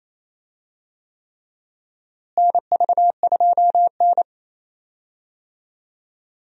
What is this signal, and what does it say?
Morse code, heard as a keyed tone, about 28 wpm, 700 Hz. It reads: NV2N
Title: USA Call Signs - 4 Characters - Repeated 3x - 28wpm